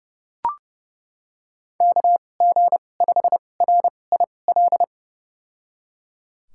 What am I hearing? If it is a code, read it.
Morse code, heard as a keyed tone, about 31 wpm, spaced as for 22 wpm, 700 Hz. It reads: KG5RIL
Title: USA Call Signs - 6 Characters - 2x Character Spacing - 30wpm